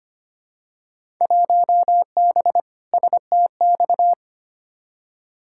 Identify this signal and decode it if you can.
Morse code, heard as a keyed tone, about 25 wpm, 700 Hz. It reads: 1B STX
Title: Field Day Exchanges Encore - 25wpm